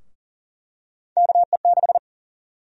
Morse code, heard as a keyed tone, about 41 wpm, 700 Hz. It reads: KE6
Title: USA Call Sign Prefixes - Repeated 2x - 40wpm